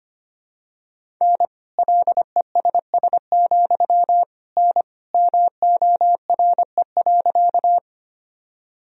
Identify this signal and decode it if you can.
Morse code, heard as a keyed tone, about 25 wpm, 700 Hz. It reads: N LESS, N MORE.
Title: Sentences from Top 400 Words and Abbreviations 25wpm